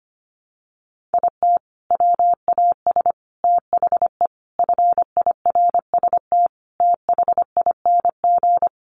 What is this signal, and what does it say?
Morse code, heard as a keyed tone, about 25 wpm, 700 Hz. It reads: IT WAS THE FIRST THING
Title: Sentences from Top 300 Words 25wpm